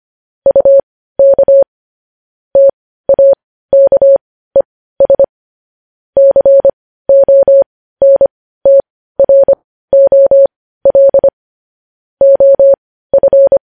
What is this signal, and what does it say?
Morse code, heard as a keyed tone, about 25 wpm, spaced as for 15 wpm, 550 Hz. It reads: UK TAKES CONTROL OF
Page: News Headlines in Morse Code at 15 WPM